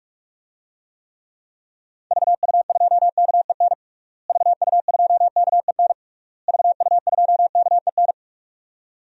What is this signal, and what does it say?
Morse code, heard as a keyed tone, about 46 wpm, 700 Hz. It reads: VU2KEN VU2KEN VU2KEN
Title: DXCC-100 Call Signs - 6 Characters - Repeated 3x - 45wpm